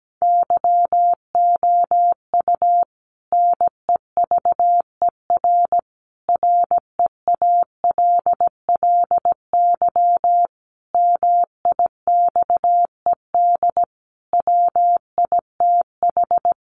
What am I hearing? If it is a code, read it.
Morse code, heard as a keyed tone, about 17 wpm, 700 Hz. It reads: YOU NEVER REALLY MIXED WITH
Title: Sentences from Top 1500 Words 17wpm